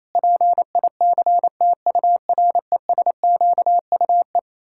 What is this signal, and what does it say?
Morse code, heard as a keyed tone, about 28 wpm, 700 Hz. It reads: PICTURESQUE
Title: Pangram Challenge v2 - 28wpm